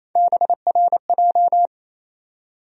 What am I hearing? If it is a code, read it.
Morse code, heard as a keyed tone, about 28 wpm, 700 Hz. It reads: BRJ
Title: USA Call Sign Postfixes - 3 Letters 28wpm